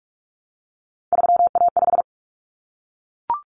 Morse code, heard as a keyed tone, about 44 wpm, 700 Hz. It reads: 3A5
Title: Common Cut Numbers - 3-Digit 45wpm